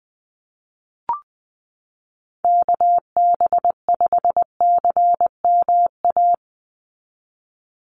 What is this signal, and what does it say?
Morse code, heard as a keyed tone, about 20 wpm, 700 Hz. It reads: KB5CMA
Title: USA Call Signs 20wpm